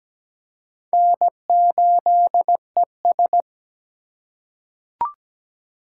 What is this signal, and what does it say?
Morse code, heard as a keyed tone, about 17 wpm, 700 Hz. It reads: N8ES
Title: USA Call Signs - 2x Character Spacing - 17wpm